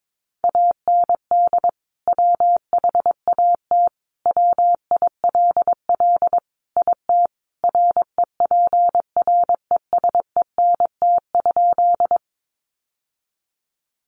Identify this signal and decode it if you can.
Morse code, heard as a keyed tone, about 22 wpm, 700 Hz. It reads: AND WHAT WILL IT REPRESENT?
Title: Sentences from Top 1200 Words 22wpm